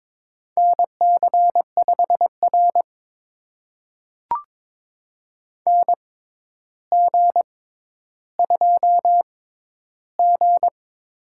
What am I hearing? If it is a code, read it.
Morse code, heard as a keyed tone, about 22 wpm, spaced as for 11 wpm, 700 Hz. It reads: NC5R NG2G
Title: USA Call Signs - 4 Characters - 6x Character Spacing - 22wpm